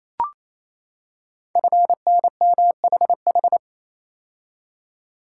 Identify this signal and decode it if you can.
Morse code, heard as a keyed tone, about 29 wpm, 700 Hz. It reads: FNMHH
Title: Code Groups - 5 Letters - 28wpm